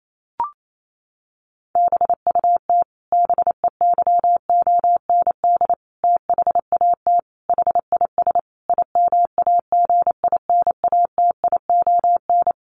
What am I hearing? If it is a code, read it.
Morse code, heard as a keyed tone, about 28 wpm, 700 Hz. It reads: BUT BEYOND THAT HIS IMAGINATION
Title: Sentences from Top 1300 Words 28wpm